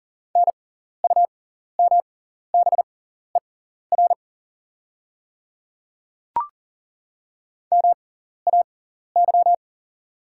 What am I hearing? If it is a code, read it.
Morse code, heard as a keyed tone, about 40 wpm, spaced as for 14 wpm, 700 Hz. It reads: NUMBER MAY
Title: 6x Character Spacing - Top 100 Words - 40wpm